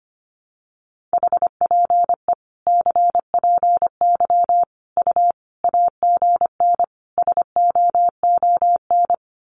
Morse code, heard as a keyed tone, about 25 wpm, 700 Hz. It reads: HPE CPY U AGN SOON
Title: Common QSO Phrases 25wpm